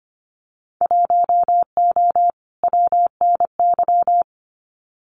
Morse code, heard as a keyed tone, about 25 wpm, 700 Hz. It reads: 1O WNY
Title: Winter Field Day Exchanges 25wpm